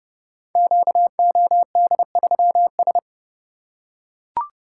Morse code, heard as a keyed tone, about 30 wpm, 700 Hz. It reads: QOD3S
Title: Code Groups - 5 Letter:Numbers - 30wpm